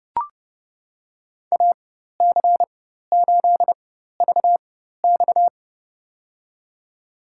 Morse code, heard as a keyed tone, about 30 wpm, spaced as for 14 wpm, 700 Hz. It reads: AC8VX
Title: USA Call Signs - 5 Characters - 4x Character Spacing - 30wpm